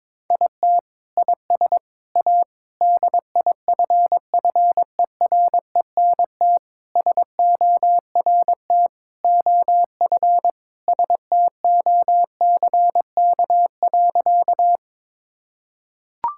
Morse code, heard as a keyed tone, about 22 wpm, 700 Hz. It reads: IT IS A DIFFERENT SORT OF STOCK.
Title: Sentences from Top 1000 Words - Review Part 5 - 22wpm